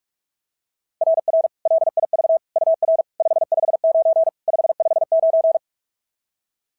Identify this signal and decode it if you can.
Morse code, heard as a keyed tone, about 44 wpm, 690 Hz. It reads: RR LIV UR 559 559
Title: SOTA QSO Practice 45wpm